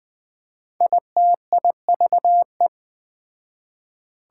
Morse code, heard as a keyed tone, about 21 wpm, 700 Hz. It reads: ITIVE
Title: Speed-Racing Prefixes and Suffixes 17wpm